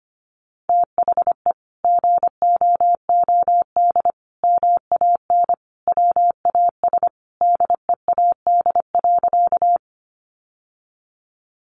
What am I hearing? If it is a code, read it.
Morse code, heard as a keyed tone, about 25 wpm, 700 Hz. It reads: THE GOOD MAN WAS DEAD.
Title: Sentences from Top 1500 Words 25wpm